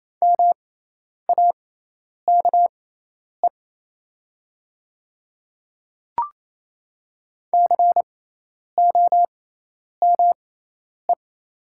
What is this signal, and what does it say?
Morse code, heard as a keyed tone, about 28 wpm, spaced as for 10 wpm, 700 Hz. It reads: MAKE COME
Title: 6x Character Spacing - Top 100 Words - 28wpm